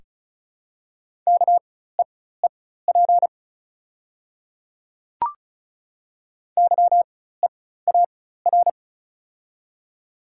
Morse code, heard as a keyed tone, about 36 wpm, spaced as for 16 wpm, 690 Hz. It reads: KEEP YEAR
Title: 4x Character Spacing - Top 300 Words - 35wpm